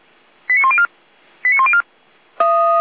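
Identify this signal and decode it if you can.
Five-tone selective call (ZVEI), heard as quick runs of five tones, tones of about 70 ms. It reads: 89104 89104
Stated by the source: ZVEI five-tone selective call 89094 received via SDR.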